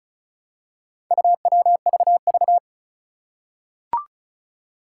Morse code, heard as a keyed tone, about 36 wpm, 700 Hz. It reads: UWVV
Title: Four Challenging Letter-Number Combinations 35wpm